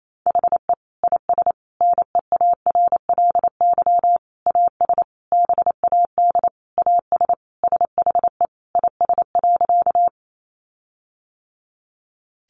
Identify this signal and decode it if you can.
Morse code, heard as a keyed tone, about 28 wpm, 700 Hz. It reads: HE IS NEARLY AS BAD AS SHE IS.
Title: Sentences from Top 1300 Words 28wpm